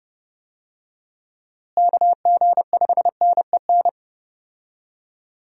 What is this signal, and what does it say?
Morse code, heard as a keyed tone, about 30 wpm, 700 Hz. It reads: KG5NEN
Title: USA Call Signs - 6 Characters - Repeated 2x - 30wpm